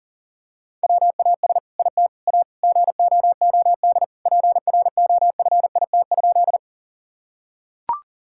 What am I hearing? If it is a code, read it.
Morse code, heard as a keyed tone, about 40 wpm, 700 Hz. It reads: WAS IT A GOOD PROFIT?
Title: Sentences from Top 1000 Words - Review Part 5 - 40wpm